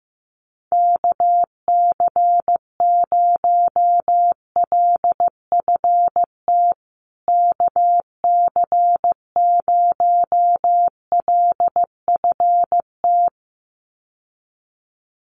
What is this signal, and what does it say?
Morse code, heard as a keyed tone, about 15 wpm, 700 Hz. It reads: KC0LFT KC0LFT
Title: USA Call Signs Repeated 2x 15wpm